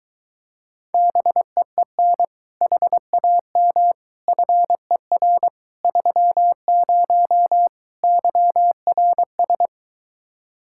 Morse code, heard as a keyed tone, about 24 wpm, 700 Hz. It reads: BEEN HAM FER 30 YRS